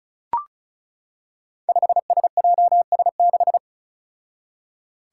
Morse code, heard as a keyed tone, about 36 wpm, 700 Hz. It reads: 5SJS6
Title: Five Challenging Letter-Number Combinations 35wpm